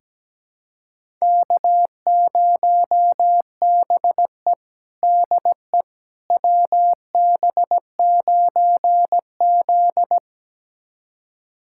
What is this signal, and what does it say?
Morse code, heard as a keyed tone, about 17 wpm, 700 Hz. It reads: K0BE DE WB9Z
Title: USA Callsign DE Callsign - 4 Characters - 17wpm